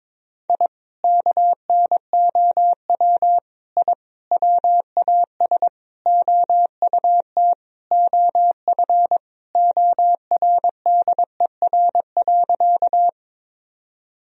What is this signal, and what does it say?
Morse code, heard as a keyed tone, about 22 wpm, 700 Hz. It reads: I KNOW I WAS OUT OF ORDER.
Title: Sentences from Top 1000 Words - Review Part 2 - 22wpm